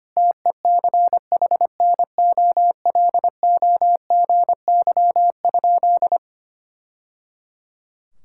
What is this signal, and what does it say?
Morse code, heard as a keyed tone, about 25 wpm, 700 Hz. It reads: TECHNOLOGY?
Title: Sentences from Top 1000 Words - Review Part 5 - 25wpm